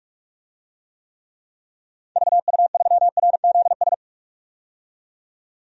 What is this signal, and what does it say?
Morse code, heard as a keyed tone, about 41 wpm, 700 Hz. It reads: VU3RZS
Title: DXCC-100 Call Signs - 6 Characters - 45wpm